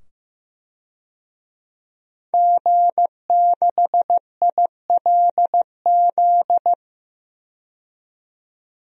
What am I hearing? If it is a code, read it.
Morse code, heard as a keyed tone, about 15 wpm, 700 Hz. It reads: G6ILZ
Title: DXCC-100 Call Signs - 5 Characters - 15wpm